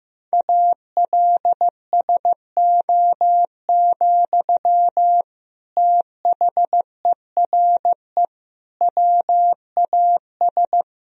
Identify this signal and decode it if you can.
Morse code, heard as a keyed tone, about 15 wpm, 700 Hz. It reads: ALSO, THERE WAS
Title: Sentences from Top 1000 Words - Review Part 1 - 15wpm